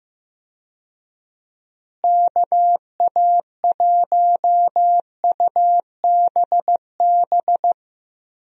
Morse code, heard as a keyed tone, about 15 wpm, 700 Hz. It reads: KA1UBB
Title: USA Call Signs - 6 Characters - Repeated 2x - 15wpm